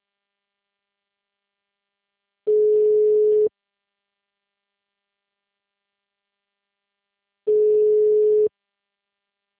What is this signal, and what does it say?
Ringback tone (Europe, 425 Hz).